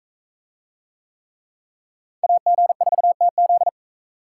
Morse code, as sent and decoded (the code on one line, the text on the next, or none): .- --. ....- - --..
AG4TZ